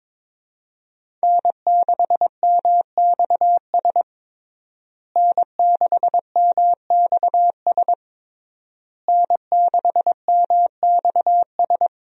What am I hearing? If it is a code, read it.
Morse code, heard as a keyed tone, about 22 wpm, 700 Hz. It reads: N6MXS N6MXS N6MXS
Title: USA Call Signs - 5 Characters - Repeated 3x - 22wpm